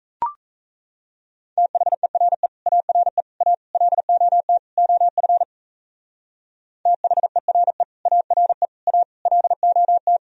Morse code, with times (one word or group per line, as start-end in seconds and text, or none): 1.6-2.5 THERE
2.7-3.2 ARE
3.4-3.5 A
3.7-4.6 LOT
4.8-5.4 OF
6.9-7.8 THERE
8.1-8.7 ARE
8.9-9.0 A
9.3-10.2 LOT